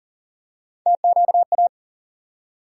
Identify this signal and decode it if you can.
Morse code, heard as a keyed tone, about 40 wpm, 700 Hz. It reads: TQA
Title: USA Call Sign Postfixes - 3 Letters 40wpm